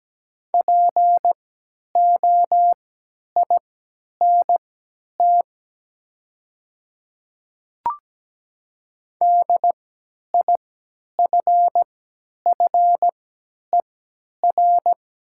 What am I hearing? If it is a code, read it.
Morse code, heard as a keyed tone, about 17 wpm, spaced as for 10 wpm, 700 Hz. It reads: POINT DIFFER
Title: 3x Character Spacing - Top 200 Words - 17wpm